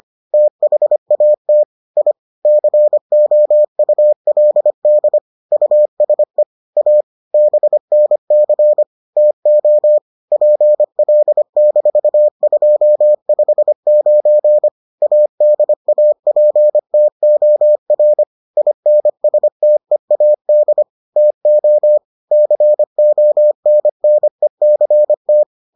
Morse code, heard as a keyed tone, about 25 wpm, 600 Hz. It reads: THAT I COULD USE A BNC TO PL-259 ADAPTOR INSTEAD TO CONNECT